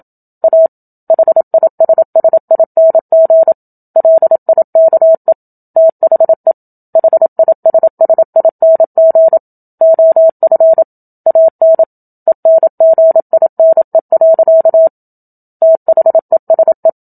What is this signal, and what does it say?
Morse code, heard as a keyed tone, about 27 wpm, 650 Hz. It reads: A HISSING LIKE THE HISSING OF AN ENGINE. THESE
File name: Book Two / 01-Chapter 1 - 27 WPM